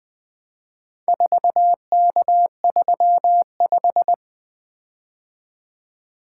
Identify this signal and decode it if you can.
Morse code, heard as a keyed tone, about 21 wpm, 700 Hz. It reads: 4K35
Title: Four Letter-Number Combinations - 20wpm